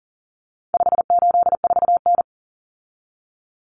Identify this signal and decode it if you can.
Morse code, heard as a keyed tone, about 40 wpm, 700 Hz. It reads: 584N
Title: Common Cut Numbers - 4-Digit 40wpm